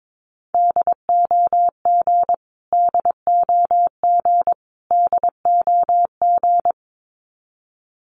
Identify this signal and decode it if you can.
Morse code, heard as a keyed tone, about 22 wpm, 700 Hz. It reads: DOG DOG DOG
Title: Top 500 Words Repeated 22wpm